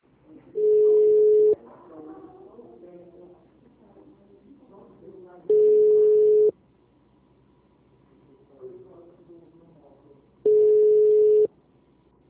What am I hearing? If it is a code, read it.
Ringback tone (Europe, 425 Hz), with speech.